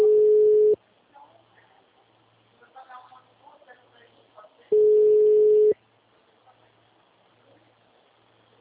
Ringback tone (Europe, 425 Hz).
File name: out- -2039- - - .30